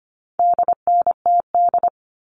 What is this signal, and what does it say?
Morse code, heard as a keyed tone, about 25 wpm, 700 Hz. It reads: DNTD
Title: All Cut Numbers - 4-Digit 25wpm